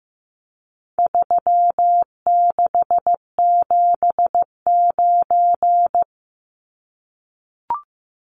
Morse code, heard as a keyed tone, about 15 wpm, 700 Hz. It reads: 3679
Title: Numbers - 4-Digit 15wpm